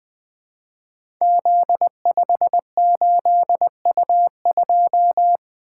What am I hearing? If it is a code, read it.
Morse code, heard as a keyed tone, about 20 wpm, 700 Hz. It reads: Z58U2
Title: Code Groups - 5 Letter:Numbers - 20wpm